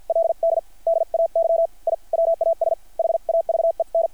Morse code, heard as a keyed tone, about 50 wpm, spaced as for 44 wpm, 660 Hz. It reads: LD DAY I WAS HAVEN
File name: archived / CRSnet